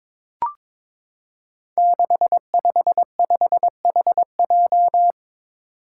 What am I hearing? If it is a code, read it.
Morse code, heard as a keyed tone, about 22 wpm, 700 Hz. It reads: 655HJ
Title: Five Challenging Letter-Number Combinations 22wpm